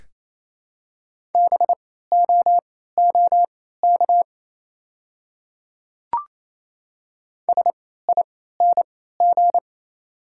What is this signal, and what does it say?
Morse code, heard as a keyed tone, about 28 wpm, spaced as for 16 wpm, 700 Hz. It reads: BOOK SING
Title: 3x Character Spacing - Top 500 Words - 28wpm